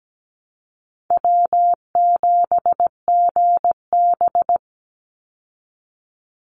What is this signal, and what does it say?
Morse code, heard as a keyed tone, about 17 wpm, 700 Hz. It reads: W7GB
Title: Easy Call Signs Repeated 3x - 17wpm